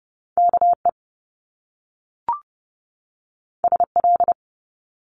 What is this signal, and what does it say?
Morse code, heard as a keyed tone, about 30 wpm, 700 Hz. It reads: KE SL
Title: Top Two Letter Combinations in Words 30wpm